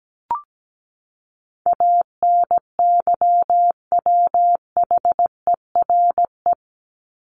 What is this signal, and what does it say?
Morse code, heard as a keyed tone, about 17 wpm, 700 Hz. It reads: ANYWHERE
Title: Top 1251-1500 Words 17wpm